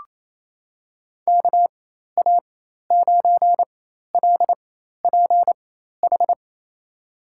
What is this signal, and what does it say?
Morse code, heard as a keyed tone, about 28 wpm, spaced as for 13 wpm, 700 Hz. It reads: KA9LPH
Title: USA Call Signs - 6 Characters - 4x Character Spacing - 28wpm